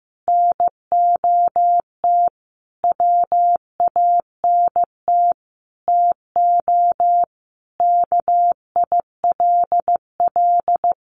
Morse code, heard as a keyed tone, about 15 wpm, 700 Hz. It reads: NOT WANT TO KILL
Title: Additional Easy Phrases 15wpm